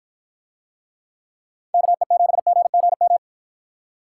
Morse code, heard as a keyed tone, about 50 wpm, 700 Hz. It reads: KE7GGM